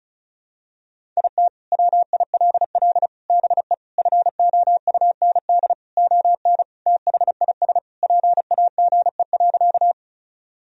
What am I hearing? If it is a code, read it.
Morse code, heard as a keyed tone, about 36 wpm, 700 Hz. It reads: IT WILL BE FOUND ON THIS PAGE.
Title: Sentences from Top 1000 Words - Review Part 1 - 35wpm